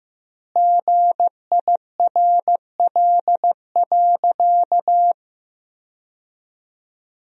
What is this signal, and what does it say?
Morse code, heard as a keyed tone, about 15 wpm, 700 Hz. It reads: GIRL.
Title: 3x Word-Spacing - Sentences from Top 400 Words - 15wpm